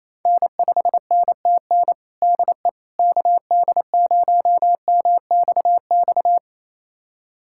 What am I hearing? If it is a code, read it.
Morse code, heard as a keyed tone, about 28 wpm, 700 Hz. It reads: N5NTN DE KD0MXX
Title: USA Callsign DE Callsign 28wpm